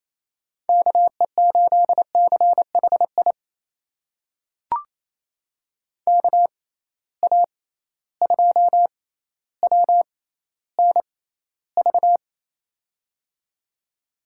Morse code, heard as a keyed tone, about 27 wpm, spaced as for 14 wpm, 700 Hz. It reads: KE8CHI KA2WNV
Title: USA Call Signs - 6 Characters - 6x Character Spacing - 28wpm